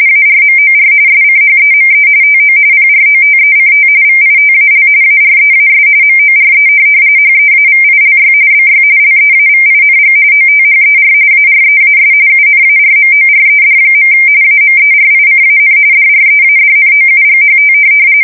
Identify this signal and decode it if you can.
RTTY, about 45.45 baud. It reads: ARMOR SCHEME TO PROTECT VITAL AREAS MORE HEAVILY BY LEAVING OTHER AREAS UNARMORED. AMERICAN DREADNOUGHT
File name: fsk170 / USS-New-Jersey